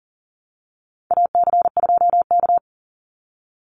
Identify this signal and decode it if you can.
Morse code, heard as a keyed tone, about 40 wpm, 700 Hz. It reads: AC2K
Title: Call Signs - 40wpm